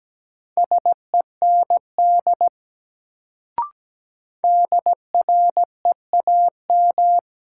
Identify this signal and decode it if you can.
Morse code, heard as a keyed tone, about 17 wpm, 700 Hz. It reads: SEND DREAM
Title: Sentences from Top 1500 Words - New Word Drill - 17wpm